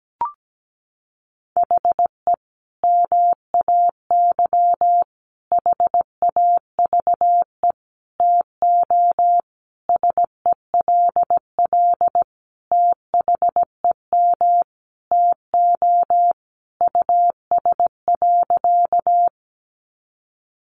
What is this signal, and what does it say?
Morse code, heard as a keyed tone, about 17 wpm, 700 Hz. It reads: HE MAY HAVE TO SELL THEM TO US.
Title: Sentences from Top 1100 Words 17wpm